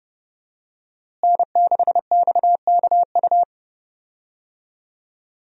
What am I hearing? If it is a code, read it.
Morse code, heard as a keyed tone, about 30 wpm, 700 Hz. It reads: N6XKU
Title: USA Call Signs - 5 Characters - Repeated 3x - 30wpm